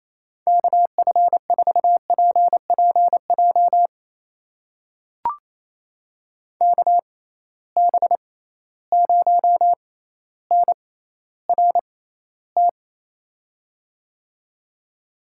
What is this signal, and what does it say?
Morse code, heard as a keyed tone, about 28 wpm, spaced as for 14 wpm, 700 Hz. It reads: KF4PPJ KB0NRT
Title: USA Call Signs - 6 Characters - 6x Character Spacing - 28wpm